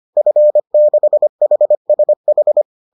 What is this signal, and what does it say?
Morse code, heard as a keyed tone, about 25 wpm, 600 Hz. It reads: F6HSH
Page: bulletin UFT sous forme texte et en CW